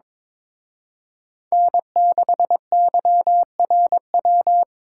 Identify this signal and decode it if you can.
Morse code, heard as a keyed tone, about 26 wpm, 700 Hz. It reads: N6YRW
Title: Speed Racing - USA 1x3 Call Signs 22wpm